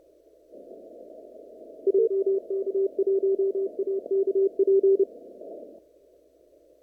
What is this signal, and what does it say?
Morse code, heard as a keyed tone, about 29 wpm, 390 Hz. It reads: JK1AKP
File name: CW / kt5j wpx cw-op-k5pi